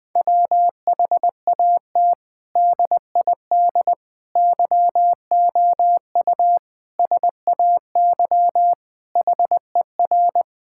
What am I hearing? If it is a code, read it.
Morse code, heard as a keyed tone, about 20 wpm, 700 Hz. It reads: WHAT DID YOU SAY HER
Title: Sentences from Top 1000 Words - Review Part 6 - 20wpm